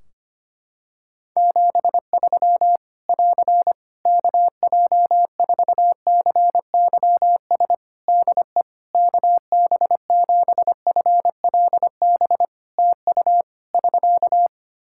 Morse code, heard as a keyed tone, about 25 wpm, 700 Hz. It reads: 73 <AR> KJ4CYS DE KB7FLB TU <SK>
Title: Practice QSO - Exchange 4 - Ending - 25wpm